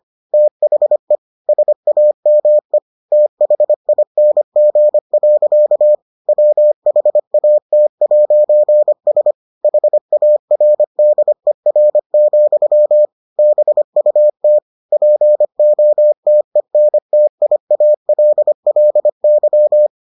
Morse code, heard as a keyed tone, about 25 wpm, 600 Hz. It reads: THE SAME THING. WHAT'S HARDER, BUT POTENTIALLY